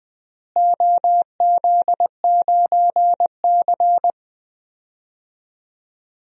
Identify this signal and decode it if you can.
Morse code, heard as a keyed tone, about 20 wpm, 700 Hz. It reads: OZ9C
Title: DXCC-100 Call Signs - 4 Characters - Repeated 2x - 20wpm